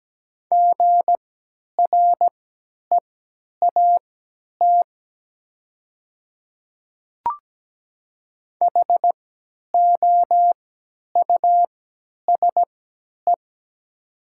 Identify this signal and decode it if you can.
Morse code, heard as a keyed tone, about 17 wpm, spaced as for 10 wpm, 700 Hz. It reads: GREAT HOUSE
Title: 3x Character Spacing - Top 200 Words - 17wpm